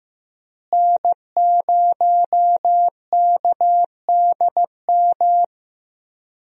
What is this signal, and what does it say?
Morse code, heard as a keyed tone, about 16 wpm, 700 Hz. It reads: N0KDM
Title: USA Call Signs - 5 Characters - Repeated 2x - 15wpm